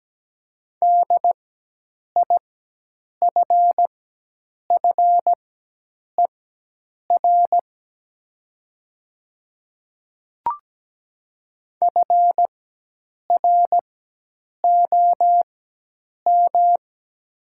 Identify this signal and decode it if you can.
Morse code, heard as a keyed tone, about 17 wpm, spaced as for 8 wpm, 700 Hz. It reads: DIFFER FROM
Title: 4x Character Spacing - Top 200 Words - 17wpm